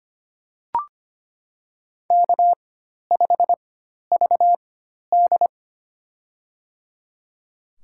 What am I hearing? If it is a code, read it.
Morse code, heard as a keyed tone, about 25 wpm, spaced as for 12 wpm, 700 Hz. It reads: K5VD
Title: USA Call Signs - 4 Characters - 4x Character Spacing - 25wpm